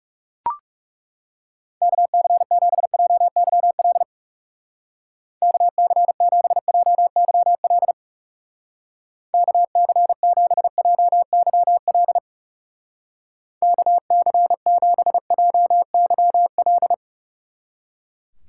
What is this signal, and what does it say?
Morse code, heard as a keyed tone, about 36 wpm, 700 Hz. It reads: KC7JYL KC7JYL KC7JYL KC7JYL
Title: Speed Racing - USA Call Signs 30wpm